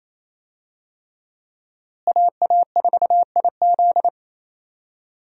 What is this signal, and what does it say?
Morse code, heard as a keyed tone, about 28 wpm, 700 Hz. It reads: AA4IZ
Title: USA Call Signs - 5 Characters - Repeated 3x - 28wpm